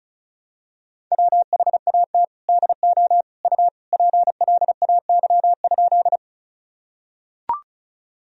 Morse code, heard as a keyed tone, about 36 wpm, 700 Hz. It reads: WHAT DO U PLAY?
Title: Sentences from Top 200 Words and Abbreviations 35wpm